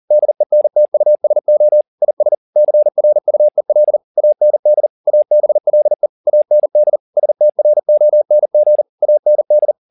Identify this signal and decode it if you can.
Morse code, heard as a keyed tone, about 40 wpm, 600 Hz. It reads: DENTUSO IS CRUEL AND ABLE AND STRONG AND
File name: the old man and the sea 2 40wpm